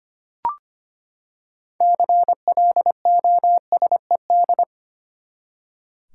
Morse code, heard as a keyed tone, about 25 wpm, 700 Hz. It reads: CLOSED
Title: Six Letter Words - 25wpm